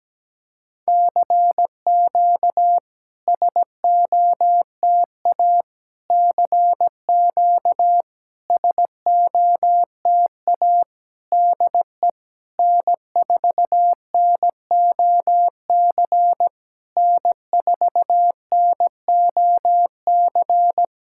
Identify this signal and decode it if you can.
Morse code, heard as a keyed tone, about 17 wpm, 700 Hz. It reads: CQ SOTA CQ SOTA DE N4NOC N4NOC
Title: SOTA QSO Practice 17wpm